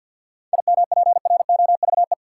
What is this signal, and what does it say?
Morse code, heard as a keyed tone, about 52 wpm, 700 Hz. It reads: IMPROVE
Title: Sentences from Top 800 Words - New Word Drill - 50wpm